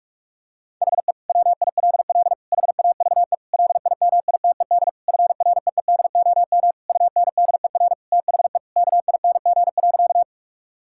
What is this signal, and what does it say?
Morse code, heard as a keyed tone, about 46 wpm, 700 Hz. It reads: HE WILL HAVE LIMITED FREEDOM UNDER THE KING.
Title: Sentences from Top 1000 Words - Review Part 6 - 45wpm